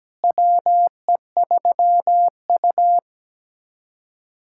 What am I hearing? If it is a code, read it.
Morse code, heard as a keyed tone, about 17 wpm, 700 Hz. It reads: WE3U
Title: USA Callsign DE Callsign - 4 Characters - 17wpm